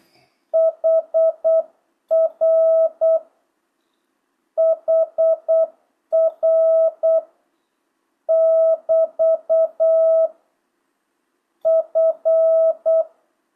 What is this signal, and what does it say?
Morse code, heard as a keyed tone, about 8 wpm, 650 Hz. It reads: HR HR <BT> F
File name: jh8cbh / 2025cw